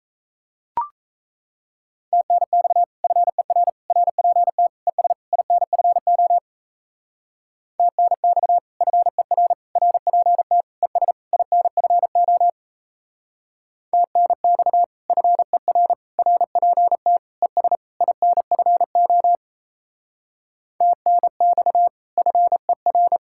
Morse code, heard as a keyed tone, about 35 wpm, 700 Hz. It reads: TNX FER RPT ES INFO TNX FER RPT ES INFO TNX FER RPT ES INFO TNX FER
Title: Speed-Racing Common QSO Phrases Encore - 28wpm